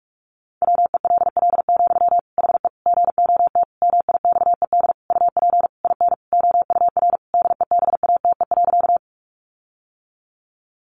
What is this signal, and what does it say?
Morse code, heard as a keyed tone, about 44 wpm, 700 Hz. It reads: WELL, HE GOT MIXED UP IN OUR DEBATE.
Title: Sentences from Top 1500 Words 45wpm